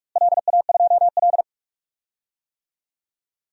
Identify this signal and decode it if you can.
Morse code, heard as a keyed tone, about 47 wpm, 700 Hz. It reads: LA2L
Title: DXCC-100 Call Signs - 4 Characters - Repeated 2x - 45wpm